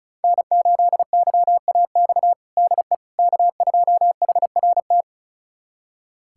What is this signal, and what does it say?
Morse code, heard as a keyed tone, about 36 wpm, 700 Hz. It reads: N8YAX DE K2HRT
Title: USA Callsign DE Callsign - 5 Characters - 35wpm